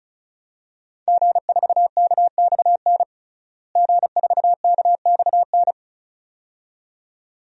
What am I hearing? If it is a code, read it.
Morse code, heard as a keyed tone, about 36 wpm, 700 Hz. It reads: G4KXN G4KXN
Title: DXCC-100 Call Signs - 5 Characters - Repeated 2x - 35wpm